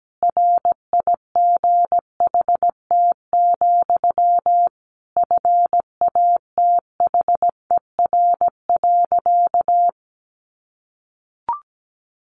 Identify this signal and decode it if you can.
Morse code, heard as a keyed tone, about 17 wpm, 700 Hz. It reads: RIGHT, FATHER.
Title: Sentences from Top 300 Words 17wpm